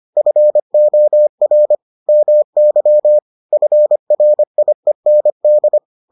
Morse code, heard as a keyed tone, about 25 wpm, 600 Hz. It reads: FOR MY FRIEND